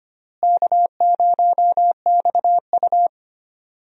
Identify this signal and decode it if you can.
Morse code, heard as a keyed tone, about 25 wpm, 700 Hz. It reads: K0XU
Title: USA Call Signs - 4 Characters - Repeated 2x - 25wpm